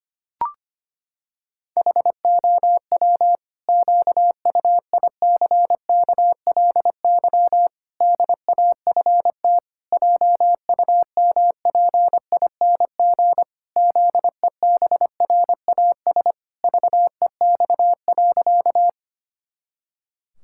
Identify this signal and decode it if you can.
Morse code, heard as a keyed tone, about 25 wpm, 700 Hz. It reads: HOW QUICKLY DAFT JUMPING ZEBRAS VEX.
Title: Pangram Challenge v2 - 25wpm